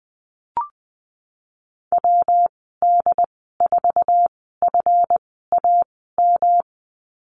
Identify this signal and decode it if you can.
Morse code, heard as a keyed tone, about 20 wpm, spaced as for 15 wpm, 700 Hz. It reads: WD4FAM
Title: 179 Lesson 29g - Slash - Call Signs 20wpm